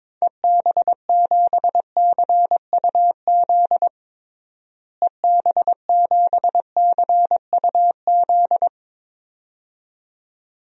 Morse code, heard as a keyed tone, about 22 wpm, 700 Hz. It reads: EB7CUZ EB7CUZ
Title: DXCC-100 Call Signs - 6 Characters - Repeated 2x - 22wpm